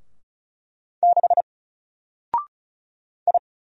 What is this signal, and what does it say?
Morse code, heard as a keyed tone, about 34 wpm, 700 Hz. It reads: 6 I
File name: Single-Letter-Number-35wpm